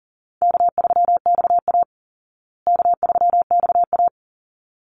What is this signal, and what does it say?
Morse code, heard as a keyed tone, about 40 wpm, 700 Hz. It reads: K3XA K3XA
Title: Easy Call Signs Repeated 2x - 40wpm